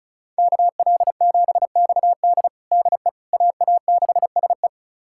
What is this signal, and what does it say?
Morse code, heard as a keyed tone, about 36 wpm, 700 Hz. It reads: KL7XD DE AA6SE
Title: USA Callsign DE Callsign - 5 Characters - 35wpm